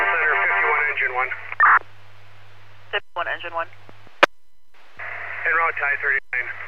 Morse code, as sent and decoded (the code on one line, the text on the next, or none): none